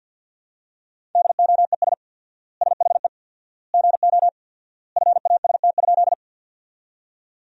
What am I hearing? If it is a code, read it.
Morse code, heard as a keyed tone, about 50 wpm, 700 Hz. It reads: DOES SHE GO FAST?
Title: 4x Word-Spacing - Sentences from Top 400 Words - 50wpm